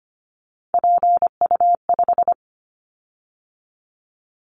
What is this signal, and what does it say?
Morse code, heard as a keyed tone, about 25 wpm, 700 Hz. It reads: PU5
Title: Callsign Prefixes for Top 50 DXCC - 25wpm